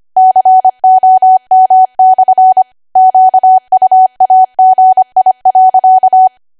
Morse code, heard as a keyed tone, about 25 wpm, 750 Hz. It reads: COM/ QUAGI.
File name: Archive / 25 WPM / 25WPM